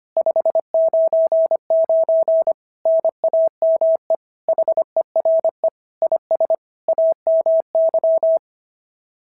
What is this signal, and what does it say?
Morse code, heard as a keyed tone, about 22 wpm, 650 Hz. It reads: 599 NAME HERE IS AMY
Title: SOTA QSO Practice 25wpm